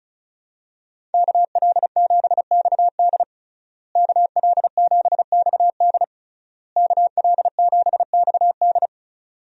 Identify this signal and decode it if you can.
Morse code, heard as a keyed tone, about 34 wpm, 700 Hz. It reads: KL7XD KL7XD KL7XD
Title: DXCC-100 Call Signs - 5 Characters - Repeated 3x - 35wpm